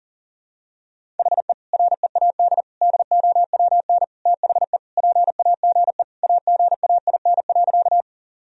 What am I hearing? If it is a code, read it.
Morse code, heard as a keyed tone, about 37 wpm, 700 Hz. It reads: HE READ DOWN THE PAGE AGAIN.
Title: Sentences from Top 1000 Words - Review Part 1 - 40wpm